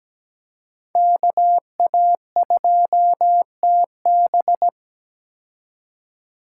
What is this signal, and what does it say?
Morse code, heard as a keyed tone, about 17 wpm, 700 Hz. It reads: KA2TB
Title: Code Groups - 5 Letter:Numbers - 17wpm